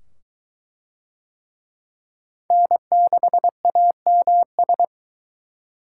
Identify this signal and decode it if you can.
Morse code, heard as a keyed tone, about 22 wpm, 700 Hz. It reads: N6AMS